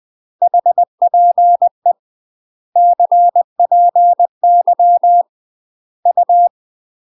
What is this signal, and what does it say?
Morse code, heard as a keyed tone, about 20 wpm, 700 Hz. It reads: HPE CPY U
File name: UK MASTER QSOs TRAINING / 20WPM / UK MASTER QSOs 0048